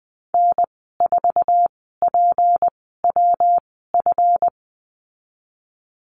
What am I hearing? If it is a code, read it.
Morse code, heard as a keyed tone, about 20 wpm, spaced as for 15 wpm, 700 Hz. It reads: N4PWF
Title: 179 Lesson 29g - Slash - Call Signs 20wpm